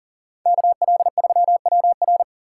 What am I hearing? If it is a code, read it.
Morse code, heard as a keyed tone, about 40 wpm, 700 Hz. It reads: KL3WR
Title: USA Call Signs - 5 Characters - 2x Character Spacing - 40wpm